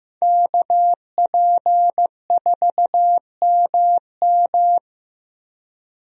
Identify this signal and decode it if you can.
Morse code, heard as a keyed tone, about 15 wpm, 700 Hz. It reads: KP4MM
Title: USA Call Signs - 5 Characters - Repeated 2x - 15wpm